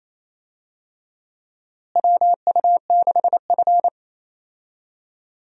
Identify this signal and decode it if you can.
Morse code, heard as a keyed tone, about 28 wpm, 700 Hz. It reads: WU6F
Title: USA Call Signs - 4 Characters - Repeated 3x - 28wpm